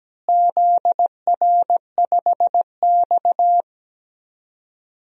Morse code, heard as a keyed tone, about 17 wpm, 700 Hz. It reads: ZR5X
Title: DXCC-100 Call Signs - 4 Characters - Repeated 3x - 17wpm